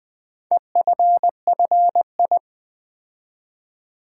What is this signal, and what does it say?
Morse code, heard as a keyed tone, about 20 wpm, 700 Hz. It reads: EFFI
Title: Build A Word - 4 Letters At A Time - 10-Letter Words - 20wpm